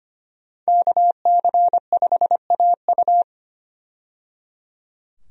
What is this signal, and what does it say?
Morse code, heard as a keyed tone, about 25 wpm, 700 Hz. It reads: KC5AU
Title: USA Call Signs - 5 Characters - 25wpm